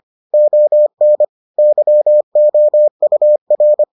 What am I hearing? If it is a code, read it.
Morse code, heard as a keyed tone, about 25 wpm, 600 Hz. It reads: ON YOUR